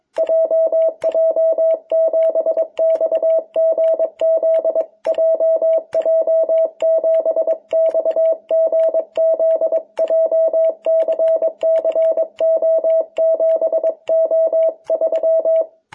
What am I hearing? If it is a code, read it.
Morse code, heard as a keyed tone, about 22 wpm, 660 Hz. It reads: JJ7XGZ JJ7XGZ JCCO7O3
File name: jh8cbh / 2021cw